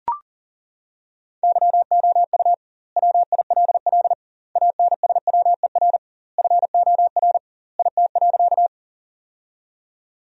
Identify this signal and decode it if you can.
Morse code, heard as a keyed tone, about 37 wpm, 700 Hz. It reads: YOU WILL ANSWER FOR IT.
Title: 2x Word-Spacing - Sentences from Top 300 Words - 40wpm